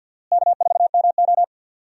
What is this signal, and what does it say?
Morse code, heard as a keyed tone, about 52 wpm, 700 Hz. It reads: K4MO
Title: USA Call Signs - 4 Characters - Repeated 3x - 50wpm